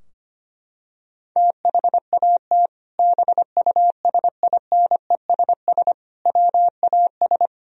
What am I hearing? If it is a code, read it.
Morse code, heard as a keyed tone, about 25 wpm, 700 Hz. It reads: THAT BUSINESS WAS
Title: Sentences from Top 1000 Words - Review Part 3 - 25wpm